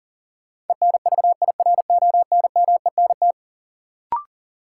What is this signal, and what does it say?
Morse code, heard as a keyed tone, about 40 wpm, 700 Hz. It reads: ENVIRONMENT
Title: Sentences from Top 1000 Words - Review Part 5 - New Word Drill - 40wpm